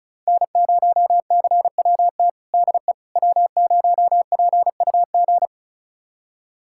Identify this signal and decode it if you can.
Morse code, heard as a keyed tone, about 34 wpm, 700 Hz. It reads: N0CWT DE W0PUG
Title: USA Callsign DE Callsign - 5 Characters - 35wpm